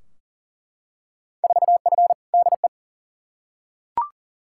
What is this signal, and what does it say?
Morse code, heard as a keyed tone, about 40 wpm, 700 Hz. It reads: 4F DE
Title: Field Day Exchanges Encore - 40wpm